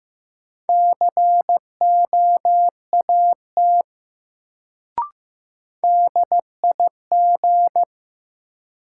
Morse code, heard as a keyed tone, about 15 wpm, 700 Hz. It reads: COAT DIG
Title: Sentences from Top 1600 Words - New Word Drill - 15wpm